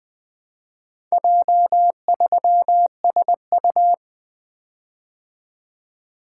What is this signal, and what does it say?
Morse code, heard as a keyed tone, about 20 wpm, 700 Hz. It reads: J3SU
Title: Four Letter-Number Combinations - 20wpm